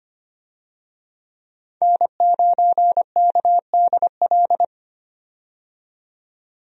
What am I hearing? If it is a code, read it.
Morse code, heard as a keyed tone, about 25 wpm, 700 Hz. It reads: N9KDL
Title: USA Call Signs - 5 Characters - 25wpm